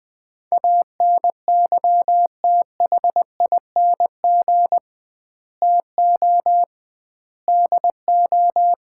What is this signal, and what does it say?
Morse code, heard as a keyed tone, about 20 wpm, 700 Hz. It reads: ANYTHING TO DO
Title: 2x Word-Spacing - Sets of 4 Words - 20wpm